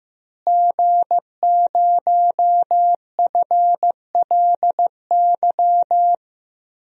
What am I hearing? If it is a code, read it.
Morse code, heard as a keyed tone, about 15 wpm, 700 Hz. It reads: G0FLY
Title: DXCC-100 Call Signs - 5 Characters - Repeated 3x - 15wpm